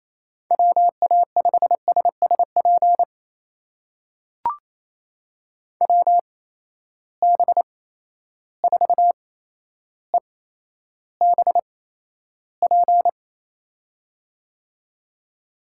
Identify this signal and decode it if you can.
Morse code, heard as a keyed tone, about 28 wpm, spaced as for 12 wpm, 700 Hz. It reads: WA5SSP WB4EBP
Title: USA Call Signs - 6 Characters - 8x Character Spacing - 28wpm